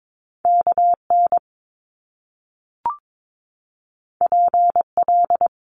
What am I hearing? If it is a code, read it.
Morse code, heard as a keyed tone, about 22 wpm, 700 Hz. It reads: KN PL
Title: Top Two Letter Combinations in Words 22wpm